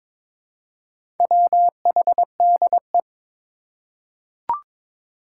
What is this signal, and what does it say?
Morse code, heard as a keyed tone, about 21 wpm, 700 Hz. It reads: WHDE
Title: Four Challenging Letter-Number Combinations 22wpm